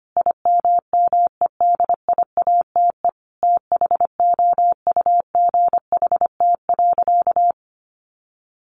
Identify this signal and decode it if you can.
Morse code, heard as a keyed tone, about 25 wpm, 700 Hz. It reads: IMMEDIATE THOUGHT.
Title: Sentences from Top 1500 Words 25wpm